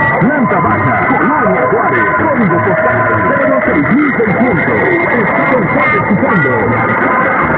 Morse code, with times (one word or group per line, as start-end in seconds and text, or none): none